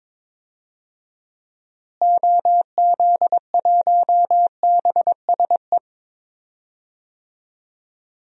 Morse code, heard as a keyed tone, about 22 wpm, 700 Hz. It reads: OZ1BSE
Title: DXCC-100 Call Signs - 6 Characters - Repeated 3x - 22wpm